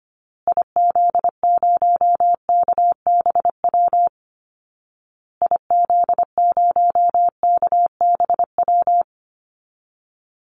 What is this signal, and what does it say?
Morse code, heard as a keyed tone, about 25 wpm, 700 Hz. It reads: IZ0KBW IZ0KBW
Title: Call Signs Repeated 2x - 25wpm